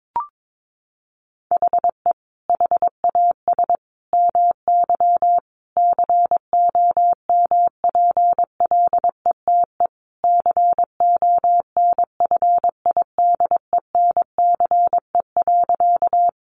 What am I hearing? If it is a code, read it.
Morse code, heard as a keyed tone, about 22 wpm, 700 Hz. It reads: HE HAS MY COMPLETE CONFIDENCE.
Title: Sentences from Top 1500 Words 22wpm